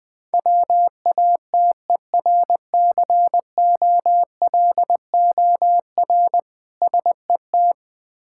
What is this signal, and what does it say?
Morse code, heard as a keyed tone, about 20 wpm, 690 Hz. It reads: WATERCOLOR SET
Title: Difficult Words with Hints - 20wpm